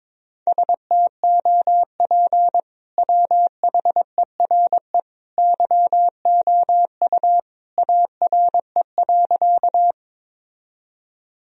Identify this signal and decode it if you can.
Morse code, heard as a keyed tone, about 22 wpm, 700 Hz. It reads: STOP WHERE YOU ARE.
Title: Sentences from Top 1000 Words - Review Part 1 - 22wpm